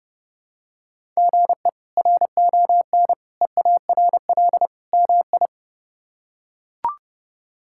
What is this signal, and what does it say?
Morse code, heard as a keyed tone, about 32 wpm, 700 Hz. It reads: GE RON EARL MI
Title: K1USN SST QSO Practice - 30wpm